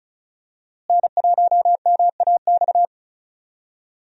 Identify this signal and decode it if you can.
Morse code, heard as a keyed tone, about 36 wpm, 700 Hz. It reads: N1MAX
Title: USA Call Signs - 5 Characters - Repeated 3x - 35wpm